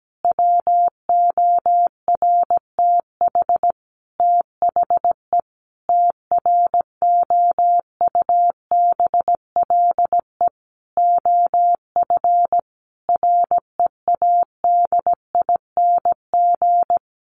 Morse code, heard as a keyed tone, about 17 wpm, 700 Hz. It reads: WORTH THE TROUBLE OF READING
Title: Sentences from Top 1200 Words 17wpm